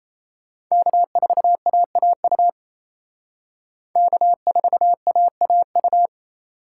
Morse code, heard as a keyed tone, about 35 wpm, 700 Hz. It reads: K4AAU K4AAU